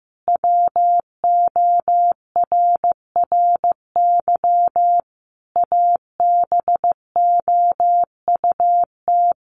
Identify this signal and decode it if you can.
Morse code, heard as a keyed tone, about 14 wpm, 700 Hz. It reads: WORRY ABOUT
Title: Easy Phrases 15wpm